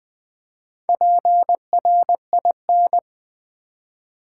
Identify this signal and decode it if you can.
Morse code, heard as a keyed tone, about 20 wpm, 700 Hz. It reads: PRIN
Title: Build A Word - 4 Letters At A Time - 8-Letter Words - 20wpm